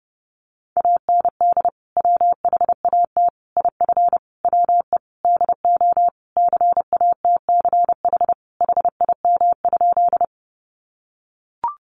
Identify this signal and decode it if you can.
Morse code, heard as a keyed tone, about 30 wpm, 700 Hz. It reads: AND WHAT IF WE DO CATCH HIM?
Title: Sentences from Top 1500 Words 30wpm